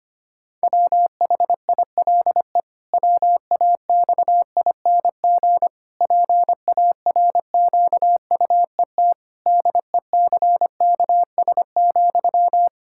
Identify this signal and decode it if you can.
Morse code, heard as a keyed tone, about 25 wpm, 700 Hz. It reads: WHILE WAXING PARQUET DECKS,
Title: Pangram Challenge v2 - 25wpm